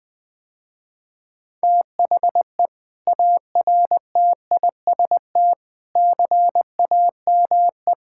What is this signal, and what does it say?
Morse code, heard as a keyed tone, about 19 wpm, 700 Hz. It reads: THE ARTIST CAME
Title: Sentences with 6-Word Phrases - Spoken Beginning - 20wpm